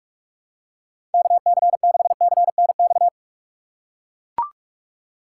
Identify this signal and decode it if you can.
Morse code, heard as a keyed tone, about 43 wpm, 700 Hz. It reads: KC6CNX
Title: USA Call Signs - 6 Characters - 4x Character Spacing - 45wpm